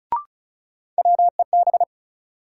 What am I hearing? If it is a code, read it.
Morse code, heard as a keyed tone, about 36 wpm, 700 Hz. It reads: WEB
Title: Sentences from Top 1000 Words - Review Part 4 - New Word Drill - 35wpm